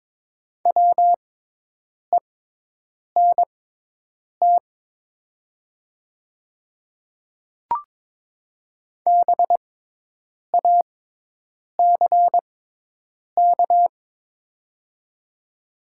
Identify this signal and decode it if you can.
Morse code, heard as a keyed tone, about 22 wpm, spaced as for 8 wpm, 700 Hz. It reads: WENT BACK
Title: 6x Character Spacing - Top 300 Words - 22wpm